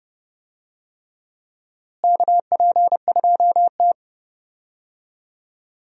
Morse code, heard as a keyed tone, about 29 wpm, 700 Hz. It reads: KP2T
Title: DXCC-100 Call Signs - 4 Characters - Repeated 3x - 30wpm